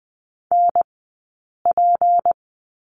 Morse code, heard as a keyed tone, about 20 wpm, spaced as for 8 wpm, 700 Hz. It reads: NP
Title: 151 Lesson 25d - Letter p - Two Characters 20wpm